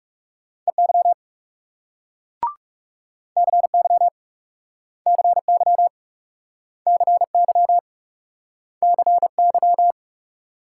Morse code, heard as a keyed tone, about 40 wpm, 700 Hz. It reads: EY CY CY CY CY